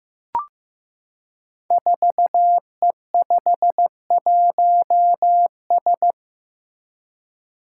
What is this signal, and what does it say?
Morse code, heard as a keyed tone, about 15 wpm, 700 Hz. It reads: 4E51S
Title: Five Challenging Letter-Number Combinations 15wpm